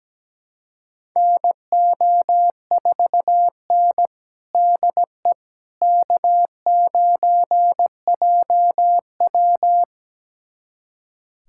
Morse code, heard as a keyed tone, about 17 wpm, 700 Hz. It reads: NO4N DE K9JW
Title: USA Callsign DE Callsign - 4 Characters - 17wpm